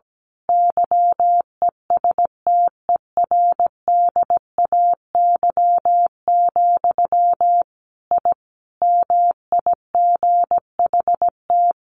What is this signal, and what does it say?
Morse code, heard as a keyed tone, about 17 wpm, 700 Hz. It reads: YESTERDAY, I MIGHT
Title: A Sound Of Thunder 17wpm